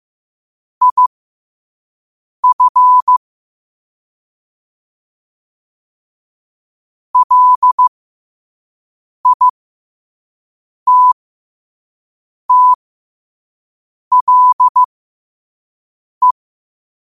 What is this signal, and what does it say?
Morse code, heard as a keyed tone, about 15 wpm, spaced as for 5 wpm, 1000 Hz. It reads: IF LITTLE
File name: qotdfiles / 5wpm